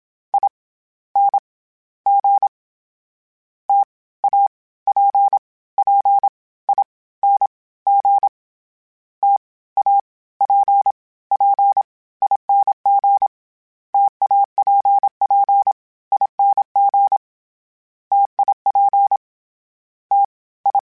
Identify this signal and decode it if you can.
Morse code, heard as a keyed tone, about 27 wpm, 800 Hz. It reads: I N G T A P P I N G T A P P ING TAPPING TIP T I
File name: Difficult Combos / Difficult Combos-mp3s / T AOEI @25